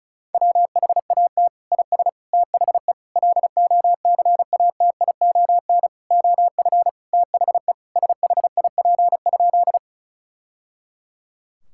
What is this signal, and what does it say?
Morse code, heard as a keyed tone, about 34 wpm, 700 Hz. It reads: WHAT IS THE LOCATION OF THE SHIP?
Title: Sentences from Top 1000 Words - Review Part 6 - 35wpm